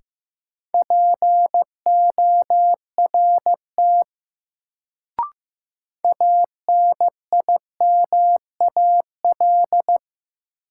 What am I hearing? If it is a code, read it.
Morse code, heard as a keyed tone, about 15 wpm, 700 Hz. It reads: PORT ANIMAL
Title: Sentences from Top 1000 Words - Review Part 1 - New Word Drill - 15wpm